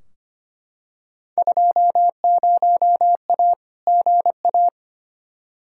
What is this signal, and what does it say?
Morse code, heard as a keyed tone, about 25 wpm, 700 Hz. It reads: 20A GA
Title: Field Day Exchanges Encore - 25wpm